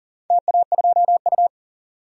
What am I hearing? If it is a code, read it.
Morse code, heard as a keyed tone, about 42 wpm, 700 Hz. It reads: TA2U
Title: DXCC-100 Call Signs - 4 Characters - 40wpm